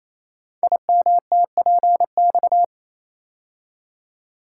Morse code, heard as a keyed tone, about 27 wpm, 700 Hz. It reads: IMTPX
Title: Code Groups - 5 Letters - 28wpm